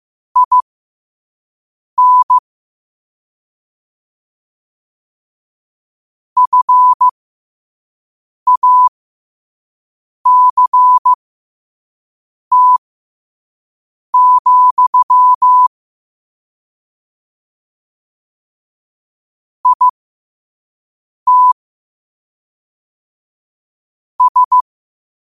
Morse code, heard as a keyed tone, about 15 wpm, spaced as for 5 wpm, 1000 Hz. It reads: IN FACT, IT S
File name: qotdfiles / 5wpm